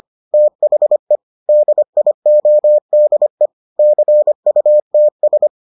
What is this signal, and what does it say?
Morse code, heard as a keyed tone, about 25 wpm, 600 Hz. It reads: THE DIODE CUTS